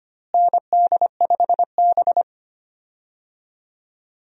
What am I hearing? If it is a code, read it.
Morse code, heard as a keyed tone, about 25 wpm, 700 Hz. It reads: ND5B
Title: USA Call Signs - 4 Characters - Repeated 3x - 25wpm